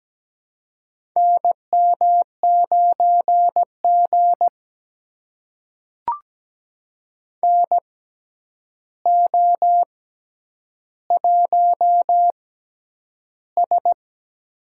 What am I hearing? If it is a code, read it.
Morse code, heard as a keyed tone, about 17 wpm, spaced as for 9 wpm, 700 Hz. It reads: NM9G NO1S
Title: USA Call Signs - 4 Characters - 6x Character Spacing - 17wpm